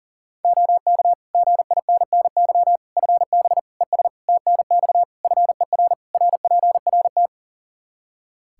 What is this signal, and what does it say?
Morse code, heard as a keyed tone, about 40 wpm, 700 Hz. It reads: OK GINNY FB ES TNX FER RPRT
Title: Practice QSO - Exchange 2 - Info - 40wpm